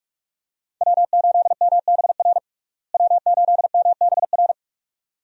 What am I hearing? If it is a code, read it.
Morse code, heard as a keyed tone, about 42 wpm, 700 Hz. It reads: W8MBR W8MBR
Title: USA Call Signs - 5 Characters - Repeated 2x - 45wpm